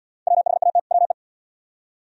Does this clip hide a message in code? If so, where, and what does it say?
Yes: Morse code from 0.3 to 1.1 s, heard as a keyed tone, about 67 wpm, 700 Hz. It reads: WHAT WE